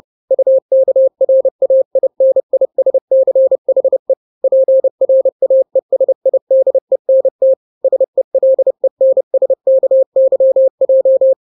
Morse code, heard as a keyed tone, about 30 wpm, 530 Hz. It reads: UKRAINISCHE PRAESIDENT SELENSKYJ
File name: daily / 29wpm